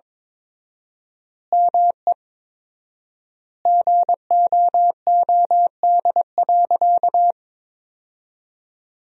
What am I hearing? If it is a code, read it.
Morse code, heard as a keyed tone, about 22 wpm, 700 Hz. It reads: ME GOOD.
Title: 4x Word-Spacing - Sentences from Top 200 Words - 22wpm